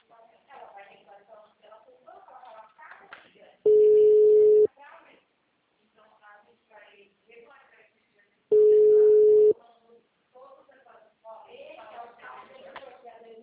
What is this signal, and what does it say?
Ringback tone (Europe, 425 Hz), with speech.
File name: out- -2021- - - .3202